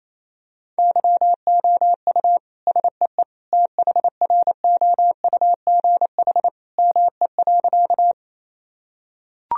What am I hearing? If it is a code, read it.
Morse code, heard as a keyed tone, about 28 wpm, 700 Hz. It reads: YOU SEE THROUGH ME.
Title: Sentences from Top 1000 Words - Review Part 1 - 28wpm